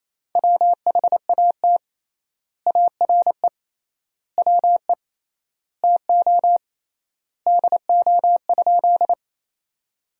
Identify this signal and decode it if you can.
Morse code, heard as a keyed tone, about 28 wpm, 700 Hz. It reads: WHAT ARE WE TO DO?
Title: 3x Word-Spacing - Sentences from Top 100 Words - 28wpm